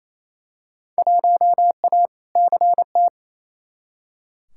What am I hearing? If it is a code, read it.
Morse code, heard as a keyed tone, about 28 wpm, 700 Hz. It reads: 1A CT
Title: Field Day Exchanges Encore - 28wpm